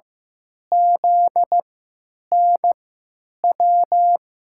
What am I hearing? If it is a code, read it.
Morse code, heard as a keyed tone, about 15 wpm, spaced as for 9 wpm, 700 Hz. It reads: ZNW
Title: USA Call Signs - 6 Characters - 3x Character Spacing - 15wpm